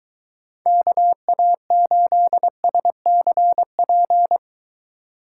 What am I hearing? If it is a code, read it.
Morse code, heard as a keyed tone, about 22 wpm, 700 Hz. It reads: KA8SCP